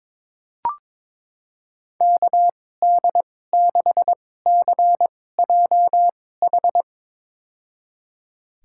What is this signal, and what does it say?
Morse code, heard as a keyed tone, about 22 wpm, spaced as for 16 wpm, 700 Hz. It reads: KD6CJH
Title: USA Call Signs - 2x Character Spacing - 22wpm